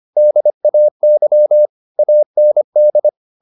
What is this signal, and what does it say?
Morse code, heard as a keyed tone, about 25 wpm, 600 Hz. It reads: DAY AND
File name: the old man and the sea part 1 25wpm